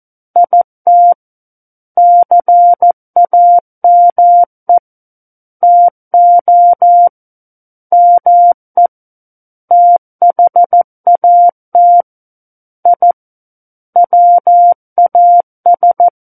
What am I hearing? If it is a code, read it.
Morse code, heard as a keyed tone, about 14 wpm, 700 Hz. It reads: IT CAME TO ME THAT I WAS
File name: Book One / 05-Chapter 5 - 14 WPM